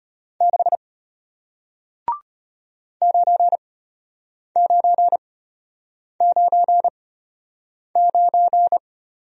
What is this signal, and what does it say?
Morse code, heard as a keyed tone, about 33 wpm, spaced as for 8 wpm, 700 Hz. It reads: 6 9999